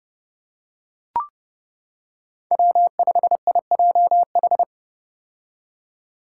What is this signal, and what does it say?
Morse code, heard as a keyed tone, about 30 wpm, 700 Hz. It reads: W5IJH
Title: K1USN SST QSO Practice - 30wpm